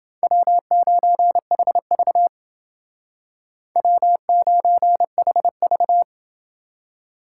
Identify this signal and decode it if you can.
Morse code, heard as a keyed tone, about 29 wpm, 700 Hz. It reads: W9HV W9HV